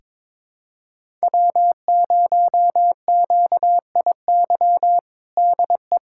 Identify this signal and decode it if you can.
Morse code, heard as a keyed tone, about 22 wpm, 700 Hz. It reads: W0QIY DE
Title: USA Callsign DE Callsign - 5 Characters - 22wpm